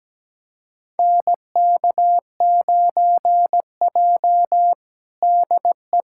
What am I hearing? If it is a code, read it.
Morse code, heard as a keyed tone, about 17 wpm, 700 Hz. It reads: NK9J DE
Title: USA Callsign DE Callsign - 4 Characters - 17wpm